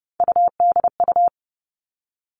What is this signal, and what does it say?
Morse code, heard as a keyed tone, about 30 wpm, 700 Hz. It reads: UDU
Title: Three Challenging Letter-Number Combinations 30wpm